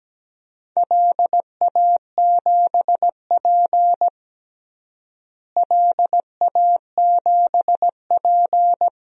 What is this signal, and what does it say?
Morse code, heard as a keyed tone, about 17 wpm, 700 Hz. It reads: LA7P LA7P
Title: DXCC-100 Call Signs - 4 Characters - Repeated 2x - 17wpm